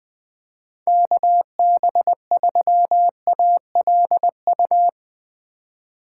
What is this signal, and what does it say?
Morse code, heard as a keyed tone, about 20 wpm, 700 Hz. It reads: KB3ALU
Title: USA Call Signs - 6 Characters - 3x Character Spacing - 20wpm